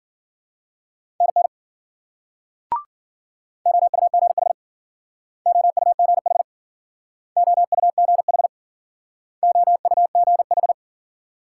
Morse code, heard as a keyed tone, about 50 wpm, 700 Hz. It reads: NN OUGH OUGH OUGH OUGH